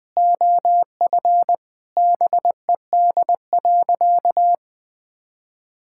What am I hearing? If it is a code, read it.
Morse code, heard as a keyed tone, about 20 wpm, 700 Hz. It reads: OF BED.
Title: Sentences with 3-Word Phrases - Spoken Beginning - 20wpm